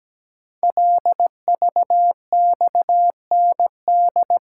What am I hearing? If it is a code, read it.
Morse code, heard as a keyed tone, about 17 wpm, 700 Hz. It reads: LVXND
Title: Code Groups - 5 Letters - 17wpm